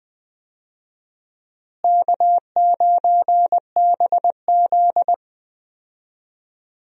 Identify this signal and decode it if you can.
Morse code, heard as a keyed tone, about 19 wpm, 700 Hz. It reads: K9BZ
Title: USA Call Signs - 4 Characters - Repeated 3x - 20wpm